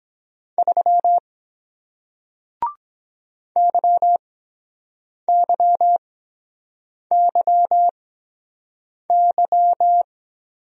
Morse code, heard as a keyed tone, about 23 wpm, spaced as for 7 wpm, 700 Hz. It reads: 3 YYYY